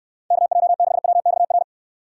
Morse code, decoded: KJ6WXR

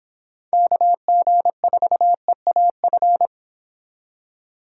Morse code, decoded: KG4EAF

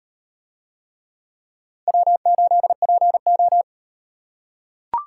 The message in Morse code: W8PO

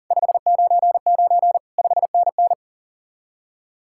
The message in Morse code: 599 5NN

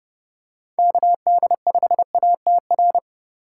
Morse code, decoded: KD5ATR

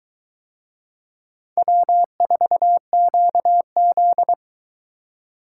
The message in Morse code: W4QZ

